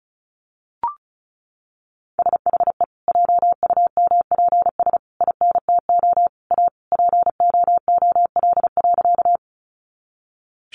Morse code, heard as a keyed tone, about 34 wpm, 700 Hz. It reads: SHE JUMPS INTO A POOL.
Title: Easy Phrases 35wpm